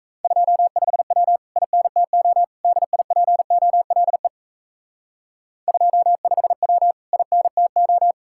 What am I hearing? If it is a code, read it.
Morse code, heard as a keyed tone, about 35 wpm, 700 Hz. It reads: 25W INTO DIPOLE 25W INTO
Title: Speed-Racing Common QSO Phrases - 28wpm